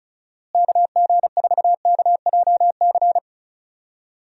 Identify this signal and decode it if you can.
Morse code, heard as a keyed tone, about 34 wpm, 700 Hz. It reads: KG4KJC
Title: USA Call Signs - 6 Characters - Repeated 2x - 35wpm